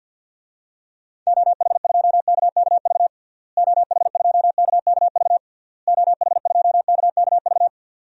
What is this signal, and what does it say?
Morse code, heard as a keyed tone, about 48 wpm, 700 Hz. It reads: OH2KKV OH2KKV OH2KKV
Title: DXCC-100 Call Signs - 6 Characters - Repeated 3x - 50wpm